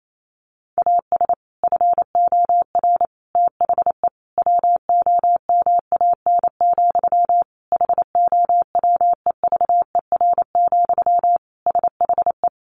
Morse code, heard as a keyed tone, about 28 wpm, 700 Hz. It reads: AS FOR THE WOMAN, HOWEVER, SHE
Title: Sentences from Top 1100 Words 28wpm